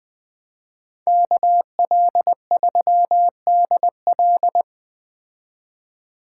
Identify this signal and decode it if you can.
Morse code, heard as a keyed tone, about 20 wpm, 700 Hz. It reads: KL3DL